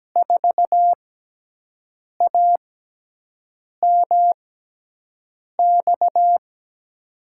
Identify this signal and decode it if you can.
Morse code, heard as a keyed tone, about 17 wpm, spaced as for 6 wpm, 700 Hz. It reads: 4AMX
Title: USA Call Signs - 6x Character Spacing - 17wpm